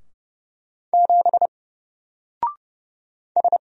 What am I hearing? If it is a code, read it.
Morse code, heard as a keyed tone, about 30 wpm, 700 Hz. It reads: 7 S
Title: Single Letter-Number Alt 30wpm